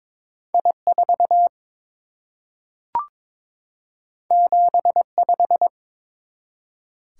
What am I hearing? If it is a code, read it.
Morse code, heard as a keyed tone, about 22 wpm, 700 Hz. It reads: I4 75
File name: practice / 22wpm / Two Letter-Number Combinations - 22wpm